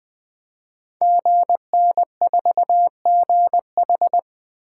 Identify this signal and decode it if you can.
Morse code, heard as a keyed tone, about 20 wpm, 700 Hz. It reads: GN4GH
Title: Code Groups - 5 Letter:Numbers - 20wpm